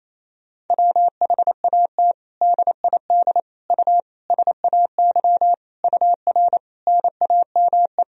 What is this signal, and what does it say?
Morse code, heard as a keyed tone, about 28 wpm, 700 Hz. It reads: WHAT DID U SAY UR NAME
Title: Sentences from Top 200 Words and Abbreviations 28wpm